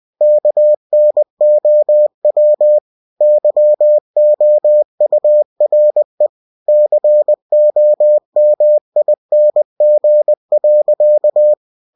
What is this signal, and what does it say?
Morse code, heard as a keyed tone, about 20 wpm, 600 Hz. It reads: KNOW YOURE COMING.
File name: cw-books / winnie-the-pooh / winnie-20wpm-page-0005